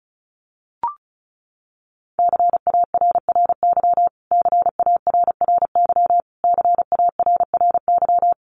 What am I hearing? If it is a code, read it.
Morse code, heard as a keyed tone, about 34 wpm, 700 Hz. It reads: CARRY CARRY CARRY
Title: Top 300 Words Repeated 35wpm